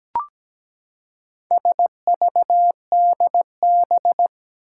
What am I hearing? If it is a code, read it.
Morse code, heard as a keyed tone, about 17 wpm, 700 Hz. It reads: SVDB
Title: Four Challenging Letter-Number Combinations 17wpm